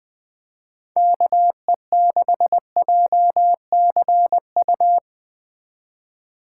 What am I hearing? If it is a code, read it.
Morse code, heard as a keyed tone, about 20 wpm, 700 Hz. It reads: KE6JCU